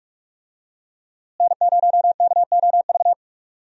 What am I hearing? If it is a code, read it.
Morse code, heard as a keyed tone, about 46 wpm, 700 Hz. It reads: N0KOV
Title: USA Call Signs - 5 Characters - Repeated 3x - 45wpm